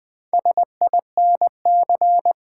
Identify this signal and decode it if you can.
Morse code, heard as a keyed tone, about 20 wpm, 700 Hz. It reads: SINC
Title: Build A Word - 1 Letter At A Time - 5-Letter Words - 20wpm